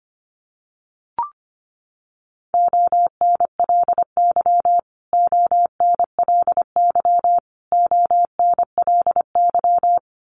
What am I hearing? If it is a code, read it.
Morse code, heard as a keyed tone, about 25 wpm, 700 Hz. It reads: ONLY ONLY ONLY
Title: Top 500 Words Repeated 25wpm